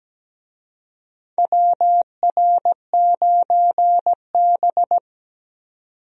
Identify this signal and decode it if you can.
Morse code, heard as a keyed tone, about 17 wpm, 700 Hz. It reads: WR9B
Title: USA Call Signs - 4 Characters - 17wpm